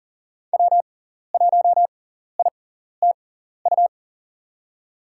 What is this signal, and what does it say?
Morse code, heard as a keyed tone, about 37 wpm, spaced as for 14 wpm, 700 Hz. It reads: W1ITU
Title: USA Call Signs - 5 Characters - 6x Character Spacing - 40wpm